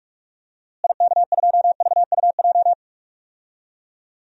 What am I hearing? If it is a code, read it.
Morse code, heard as a keyed tone, about 43 wpm, 700 Hz. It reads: IK2VUJ
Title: DXCC-100 Call Signs - 6 Characters - Repeated 3x - 45wpm